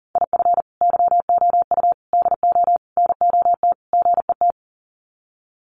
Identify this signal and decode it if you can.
Morse code, heard as a keyed tone, about 40 wpm, 700 Hz. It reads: IF YOU DO NOT GET
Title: Sets of 5 Words Encore2 40wpm